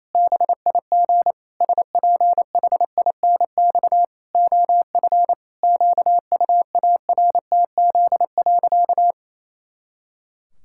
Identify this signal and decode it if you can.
Morse code, heard as a keyed tone, about 28 wpm, 700 Hz. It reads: BIG SPHINX OF QUARTZ.
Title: Pangram Challenge v2 - 28wpm